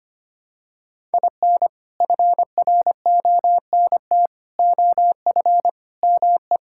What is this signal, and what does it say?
Morse code, heard as a keyed tone, about 25 wpm, 700 Hz. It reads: IN FRONT OF ME
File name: practice / 25wpm / Sentences with 4-Word Phrases - 25wpm